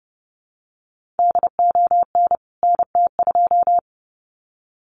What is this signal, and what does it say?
Morse code, heard as a keyed tone, about 30 wpm, 700 Hz. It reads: DON NT2
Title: CWT Exchange 30wpm